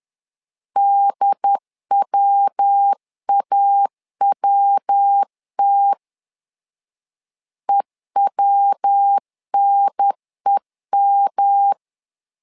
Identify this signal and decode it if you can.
Morse code, heard as a keyed tone, about 10 wpm, 800 Hz. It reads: DWAWT EWNEM
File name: LEZIONI / 3 MWRO / 3 MW 10wpm 1959